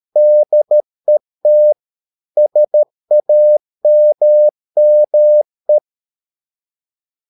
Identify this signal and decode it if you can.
Morse code, heard as a keyed tone, about 13 wpm, 600 Hz. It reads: DET SAMME
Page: Ferie 13 wpm